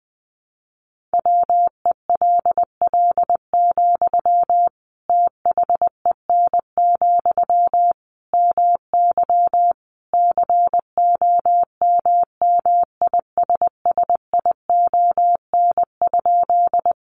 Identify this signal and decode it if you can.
Morse code, heard as a keyed tone, about 20 wpm, 700 Hz. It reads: WELL, THEN, MY COMMISSION?
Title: Sentences from Top 1400 Words 20wpm